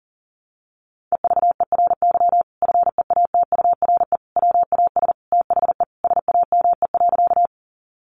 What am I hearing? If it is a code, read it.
Morse code, heard as a keyed tone, about 40 wpm, 700 Hz. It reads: EVERY FEATURE WAS THE SAME.
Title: Sentences from Top 800 Words 40wpm